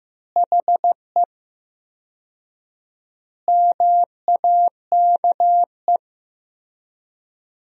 Morse code, heard as a keyed tone, about 15 wpm, 700 Hz. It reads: HE MAKE
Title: 4x Word-Spacing - Sentences from Top 400 Words - 15wpm